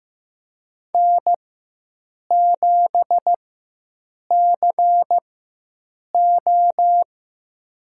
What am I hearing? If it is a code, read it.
Morse code, heard as a keyed tone, about 15 wpm, spaced as for 7 wpm, 700 Hz. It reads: N7CO
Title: USA Call Signs - 4 Characters - 4x Character Spacing - 15wpm